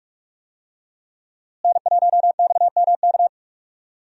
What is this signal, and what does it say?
Morse code, heard as a keyed tone, about 47 wpm, 700 Hz. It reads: N1XMK
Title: USA Call Signs - 5 Characters - Repeated 2x - 45wpm